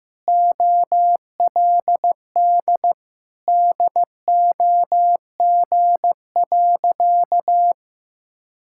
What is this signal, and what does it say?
Morse code, heard as a keyed tone, about 15 wpm, 700 Hz. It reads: OLD DOG.
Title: Sentences from Top 1000 Words - Review Part 6 - 15wpm